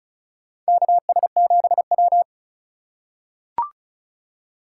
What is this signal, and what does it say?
Morse code, heard as a keyed tone, about 34 wpm, 700 Hz. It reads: KS7W
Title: USA Call Signs - 4 Characters - 6x Character Spacing - 35wpm